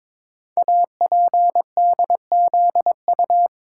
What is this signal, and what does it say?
Morse code, heard as a keyed tone, about 22 wpm, 700 Hz. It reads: APDZU
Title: Code Groups - 5 Letters - 22wpm